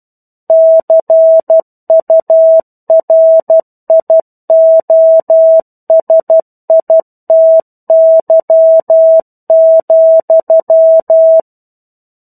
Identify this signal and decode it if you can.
Morse code, heard as a keyed tone, about 12 wpm, 650 Hz. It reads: CURIOSITY,
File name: 01-Chapter-01 12 WPM